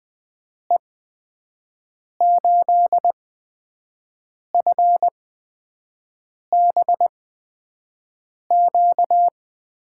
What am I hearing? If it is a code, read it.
Morse code, heard as a keyed tone, about 20 wpm, spaced as for 5 wpm, 700 Hz. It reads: E8FBQ
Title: USA Call Signs - 6 Characters - 8x Character Spacing - 20wpm